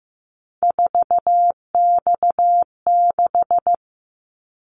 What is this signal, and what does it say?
Morse code, heard as a keyed tone, about 15 wpm, 700 Hz. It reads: 4X6
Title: Callsign Prefixes for Top 50 DXCC - 15wpm